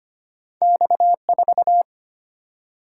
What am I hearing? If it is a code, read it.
Morse code, heard as a keyed tone, about 25 wpm, 700 Hz. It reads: X4
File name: Two Letter-Number Combinations - 25wpm